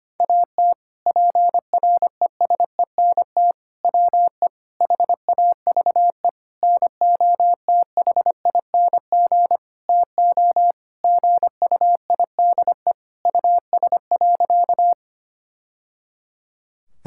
Morse code, heard as a keyed tone, about 25 wpm, 700 Hz. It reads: AT PRESENT WE HAVE NOTHING TO GUIDE US.
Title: Sentences from Top 1000 Words - Review Part 5 - 25wpm